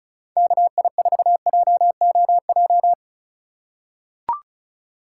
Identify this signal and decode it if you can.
Morse code, heard as a keyed tone, about 36 wpm, 700 Hz. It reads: KI4JOJ
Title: USA Call Signs - 2x Character Spacing - 35wpm